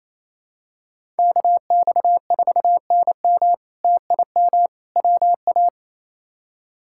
Morse code, heard as a keyed tone, about 28 wpm, 700 Hz. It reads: KX4NM TIM WA
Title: K1USN SST QSO Practice - 28wpm